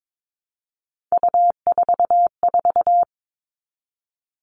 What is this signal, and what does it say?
Morse code, heard as a keyed tone, about 22 wpm, 700 Hz. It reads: U44
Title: All Cut Numbers - 3-Digit 22wpm